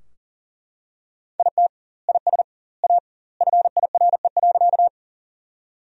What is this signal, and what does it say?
Morse code, heard as a keyed tone, about 37 wpm, 700 Hz. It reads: IT IS A FIRE.
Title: 2x Word-Spacing - Sentences from Top 400 Words - 40wpm